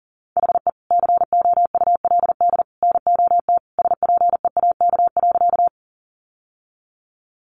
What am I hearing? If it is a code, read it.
Morse code, heard as a keyed tone, about 40 wpm, 700 Hz. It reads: HE COULD NOT SPEAK.
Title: Sentences from Top 1100 Words 40wpm